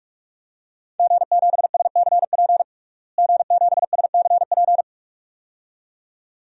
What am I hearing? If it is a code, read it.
Morse code, heard as a keyed tone, about 41 wpm, 700 Hz. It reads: G7SCP G7SCP
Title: DXCC-100 Call Signs - 5 Characters - Repeated 2x - 45wpm